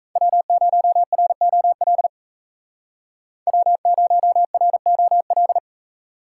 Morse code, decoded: W0ROL W0ROL

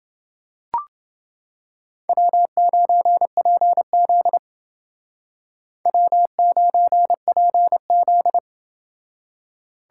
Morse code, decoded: W9PZ W9PZ

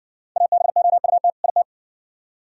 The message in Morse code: ABOUT IT

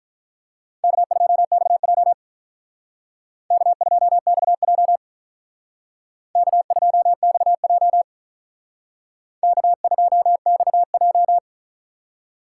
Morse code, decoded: K2XJ K2XJ K2XJ K2XJ

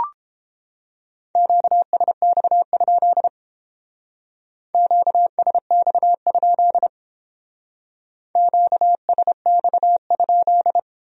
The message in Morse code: QSX? QSX? QSX?

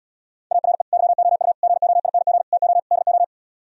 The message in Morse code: ARE YOU GOING TO DO

